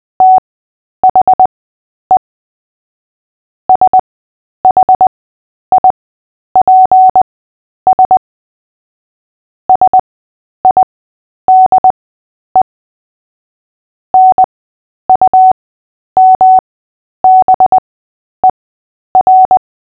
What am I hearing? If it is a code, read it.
Morse code, heard as a keyed tone, about 20 wpm, spaced as for 10 wpm, 750 Hz. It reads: THE SHIPS SIDE NUMBER
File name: cw / fists / Short-Story-3-10wpm